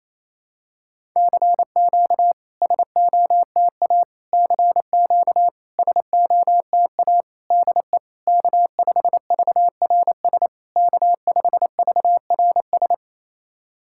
Morse code, heard as a keyed tone, about 28 wpm, 700 Hz. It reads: CQ SOTA CQ SOTA DE K5VRS K5VRS
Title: SOTA QSO Practice 28wpm